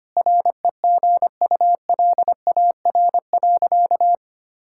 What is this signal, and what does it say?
Morse code, heard as a keyed tone, about 25 wpm, 700 Hz. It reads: REGULAR.
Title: Sentences from Top 1000 Words - Review Part 6 - 25wpm